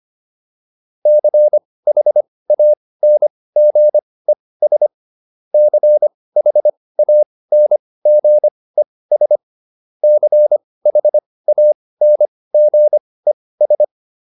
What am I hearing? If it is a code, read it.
Morse code, heard as a keyed tone, about 25 wpm, spaced as for 18 wpm, 600 Hz. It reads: CHANGES CHANGES CHANGES